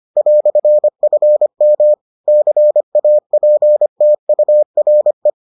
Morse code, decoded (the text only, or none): "FM CAPTURE